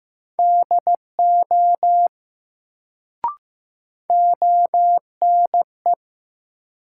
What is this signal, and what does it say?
Morse code, heard as a keyed tone, about 15 wpm, 700 Hz. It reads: DO ONE
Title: Sets of 2 Words - New Word Drill - 15wpm